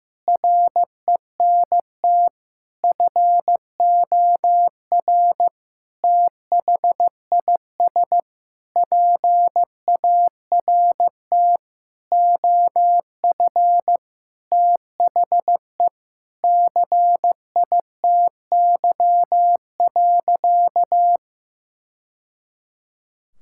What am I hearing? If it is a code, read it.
Morse code, heard as a keyed tone, about 15 wpm, 700 Hz. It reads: RENT FOR THIS PART OF THE CITY.
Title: Sentences from Top 1000 Words - Review Part 5 - 15wpm